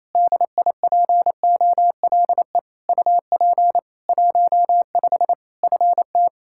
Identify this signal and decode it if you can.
Morse code, heard as a keyed tone, about 28 wpm, 700 Hz. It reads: DIPOLE UP 15 FT
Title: Practice QSO - Exchange 2 - Info - 28wpm